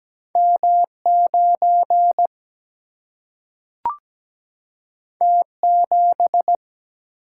Morse code, heard as a keyed tone, about 17 wpm, 700 Hz. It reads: M9 T7
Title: Single-Letter Single-Number 17wpm